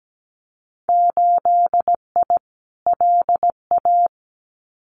Morse code, heard as a keyed tone, about 17 wpm, 700 Hz. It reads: 8I LA
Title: Winter Field Day Exchanges Encore 17wpm